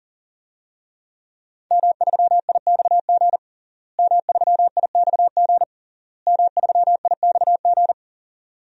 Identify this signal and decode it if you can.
Morse code, heard as a keyed tone, about 37 wpm, 700 Hz. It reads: M3IXG M3IXG M3IXG
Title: DXCC-100 Call Signs - 5 Characters - Repeated 3x - 40wpm